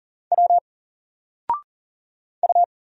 Morse code, heard as a keyed tone, about 40 wpm, 700 Hz. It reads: W U
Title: Single Letter-Number Alt 40wpm